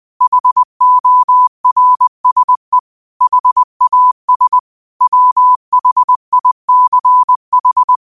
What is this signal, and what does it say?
Morse code, heard as a keyed tone, about 20 wpm, 1000 Hz. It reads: HORSE HAS WHICH
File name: qotdfiles / 20wpm